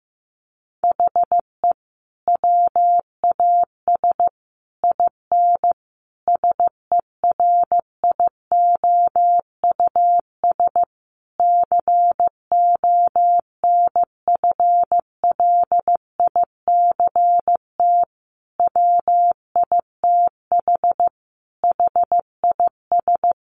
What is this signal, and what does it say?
Morse code, heard as a keyed tone, about 15 wpm, 700 Hz. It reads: HE WAS IN SERIOUS CONFLICT WITH HIS
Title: Sentences from Top 1300 Words 15wpm